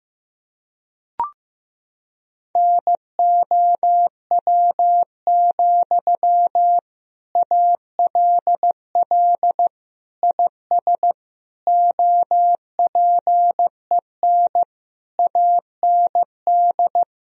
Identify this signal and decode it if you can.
Morse code, heard as a keyed tone, about 16 wpm, 700 Hz. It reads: NOW, ALL IS OPEN AND
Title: Sentences from Top 1000 Words - Review Part 4 - 15wpm